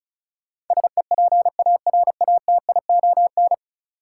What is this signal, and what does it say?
Morse code, heard as a keyed tone, about 36 wpm, 700 Hz. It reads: SEPARATION
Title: Sets of 5 Words - New Word Drill - 35wpm